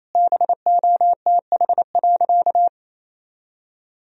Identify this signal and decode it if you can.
Morse code, heard as a keyed tone, about 28 wpm, 700 Hz. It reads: BOTH.
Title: Sentences from Top 300 Words and Abbreviations 28wpm